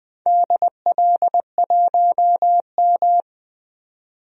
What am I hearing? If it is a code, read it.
Morse code, heard as a keyed tone, about 20 wpm, 700 Hz. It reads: DL1M
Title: DXCC-100 Call Signs - 4 Characters - 20wpm